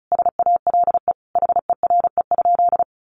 Morse code, heard as a keyed tone, about 34 wpm, 700 Hz. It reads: SALE HERE?
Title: Sentences from Top 1400 Words 35wpm